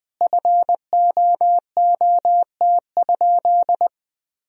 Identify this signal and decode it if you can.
Morse code, heard as a keyed tone, about 21 wpm, 700 Hz. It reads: FOOT?
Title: Sentences from Top 1000 Words - Review Part 3 - 20wpm